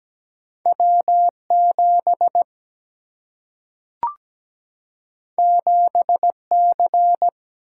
Morse code, heard as a keyed tone, about 17 wpm, 700 Hz. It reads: W7 7C
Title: Single-Letter Single-Number 17wpm